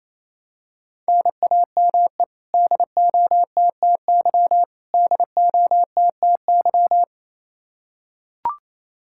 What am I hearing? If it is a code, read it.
Morse code, heard as a keyed tone, about 28 wpm, 700 Hz. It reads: NAME DOTTY DOTTY
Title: Practice QSO - Exchange 1 - Beginning - 28wpm